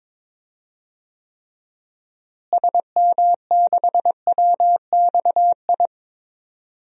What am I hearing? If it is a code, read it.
Morse code, heard as a keyed tone, about 22 wpm, 700 Hz. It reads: SM6WXI
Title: DXCC-100 Call Signs - 6 Characters - Repeated 2x - 22wpm